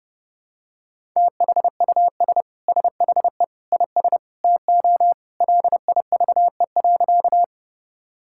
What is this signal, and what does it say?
Morse code, heard as a keyed tone, about 30 wpm, 700 Hz. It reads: THUS SHE IS TO LIVE.
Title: Sentences from Top 1000 Words - Review Part 5 - 30wpm